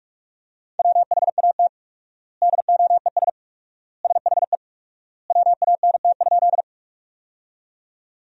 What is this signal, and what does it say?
Morse code, heard as a keyed tone, about 46 wpm, 700 Hz. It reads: WHAT DOES SHE WANT?
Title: 4x Word-Spacing - Sentences from Top 200 Words - 45wpm